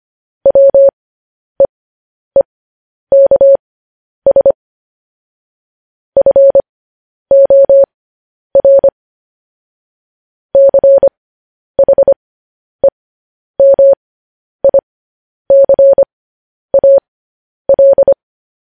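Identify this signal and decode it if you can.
Morse code, heard as a keyed tone, about 25 wpm, spaced as for 10 wpm, 550 Hz. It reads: WEEKS FOR CHEMICAL
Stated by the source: Morse code news headlines, likely sent at 10 WPM.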